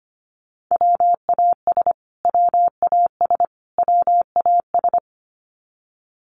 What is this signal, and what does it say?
Morse code, heard as a keyed tone, about 25 wpm, 700 Hz. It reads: WAS WAS WAS
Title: Top 300 Words Repeated 25wpm